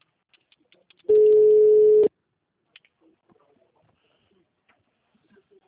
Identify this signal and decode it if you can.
Ringback tone (Europe, 425 Hz).